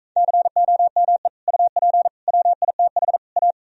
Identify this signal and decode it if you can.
Morse code, heard as a keyed tone, about 45 wpm, 700 Hz. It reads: COME UP WITH A